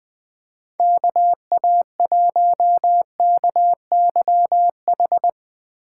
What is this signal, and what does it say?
Morse code, heard as a keyed tone, about 20 wpm, 700 Hz. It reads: KA1KYH
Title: USA Call Signs - 6 Characters - 4x Character Spacing - 20wpm